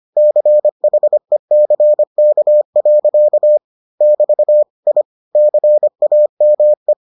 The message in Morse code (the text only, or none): CHECK. <BT> I CAME